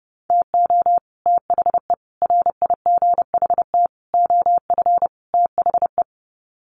Morse code, heard as a keyed tone, about 28 wpm, 700 Hz. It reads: TO THE RIGHT OF THE
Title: Sets of 5 Words Encore 30wpm